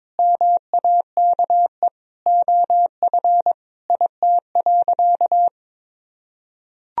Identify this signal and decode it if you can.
Morse code, heard as a keyed tone, about 22 wpm, 700 Hz. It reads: MAKE OF IT.
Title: Sentences from Top 1000 Words - Review Part 1 - 22wpm